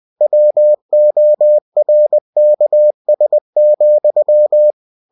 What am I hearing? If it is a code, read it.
Morse code, heard as a keyed tone, about 20 wpm, 600 Hz. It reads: WORKS,
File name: cw-books / winnie-the-pooh / winnie-20wpm-page-0113